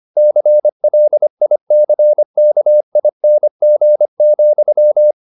Morse code, heard as a keyed tone, about 25 wpm, 600 Hz. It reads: CLICKING,